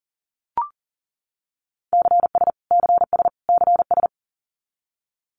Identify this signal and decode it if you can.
Morse code, heard as a keyed tone, about 40 wpm, 700 Hz. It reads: CS CS CS
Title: Abbreviations Repeated 3x - 40wpm